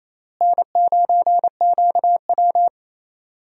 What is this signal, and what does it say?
Morse code, heard as a keyed tone, about 27 wpm, 700 Hz. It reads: N9QW
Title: USA Call Signs - 4 Characters - Repeated 2x - 28wpm